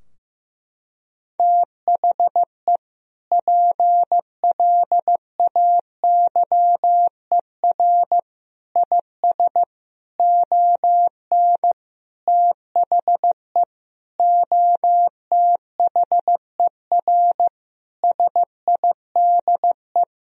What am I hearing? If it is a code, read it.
Morse code, heard as a keyed tone, about 15 wpm, 700 Hz. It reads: THE PLAYER IS ON THE OTHER SIDE
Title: Sentences from Top 1000 Words - Review Part 4 - 15wpm